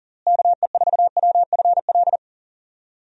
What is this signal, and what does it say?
Morse code, heard as a keyed tone, about 40 wpm, 700 Hz. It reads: KE4WFL
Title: USA Call Signs - 6 Characters - Repeated 3x - 40wpm